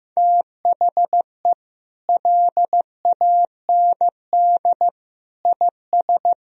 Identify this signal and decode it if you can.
Morse code, heard as a keyed tone, about 15 wpm, 700 Hz. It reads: THE LAND IS
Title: Sentences from Top 1000 Words - Review Part 2 - 15wpm